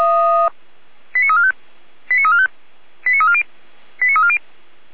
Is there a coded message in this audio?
Yes: five-tone selective call (ZVEI) from 1.1 to 4.4 s, heard as quick runs of five tones, tones of about 70 ms. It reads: 89345 89345 89340 89340